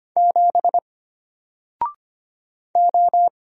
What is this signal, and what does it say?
Morse code, heard as a keyed tone, about 25 wpm, 700 Hz. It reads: 7 O
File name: Single-Letter-Number-25wpm